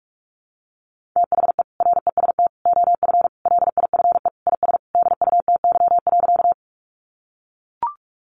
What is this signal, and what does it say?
Morse code, heard as a keyed tone, about 44 wpm, 700 Hz. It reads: THE REST OF LIFE IS DUTY.
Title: Sentences from Top 1500 Words 45wpm